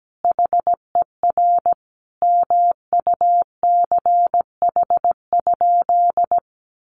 Morse code, heard as a keyed tone, about 17 wpm, 700 Hz. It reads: HER MUCH?
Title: Sentences from Top 400 Words 17wpm